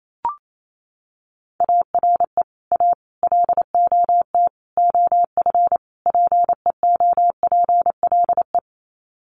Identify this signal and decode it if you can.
Morse code, heard as a keyed tone, about 28 wpm, 700 Hz. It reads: ARE A LOT OF PEOPLE
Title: Sets of 5 Words 28wpm